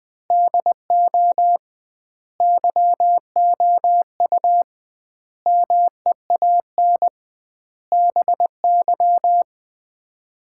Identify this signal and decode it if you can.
Morse code, heard as a keyed tone, about 20 wpm, 700 Hz. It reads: DO YOU MEAN BY
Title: 2x Word-Spacing - Sets of 5 Words - 20wpm